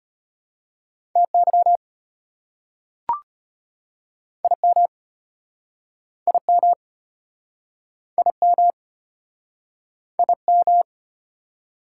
Morse code, heard as a keyed tone, about 33 wpm, 690 Hz. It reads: TY IM IM IM IM